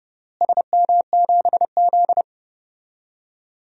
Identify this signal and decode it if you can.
Morse code, heard as a keyed tone, about 31 wpm, 700 Hz. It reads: SM7Z
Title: DXCC-100 Call Signs - 4 Characters - Repeated 2x - 30wpm